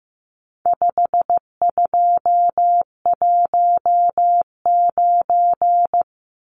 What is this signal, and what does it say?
Morse code, heard as a keyed tone, about 15 wpm, 700 Hz. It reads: 5219
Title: Numbers - 4-Digit 15wpm